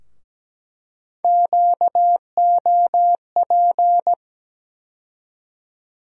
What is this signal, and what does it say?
Morse code, heard as a keyed tone, about 17 wpm, 700 Hz. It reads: QOP
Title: USA Call Sign Postfixes - 3 Letters 17wpm